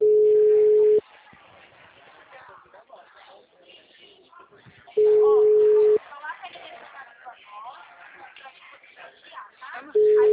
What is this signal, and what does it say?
Ringback tone (Europe, 425 Hz), with speech.
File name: out- -2002- - - .53637